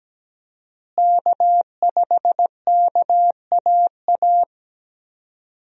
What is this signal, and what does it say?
Morse code, heard as a keyed tone, about 17 wpm, 700 Hz. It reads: K5KAA
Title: USA Call Signs - 5 Characters - Repeated 2x - 17wpm